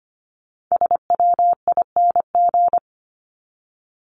Morse code, heard as a keyed tone, about 25 wpm, 700 Hz. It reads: SWING
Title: Top 1751-2000 Words 25wpm